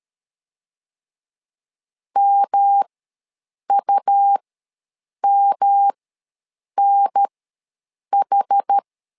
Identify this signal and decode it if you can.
Morse code, heard as a keyed tone, about 13 wpm, spaced as for 7 wpm, 800 Hz. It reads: MUMNH
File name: LEZIONI / 3 MWRO / 3 M ws 12wpm 1745